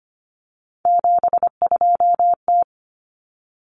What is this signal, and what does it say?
Morse code, heard as a keyed tone, about 25 wpm, 700 Hz. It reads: 72T
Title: Common Cut Numbers - 3-Digit 25wpm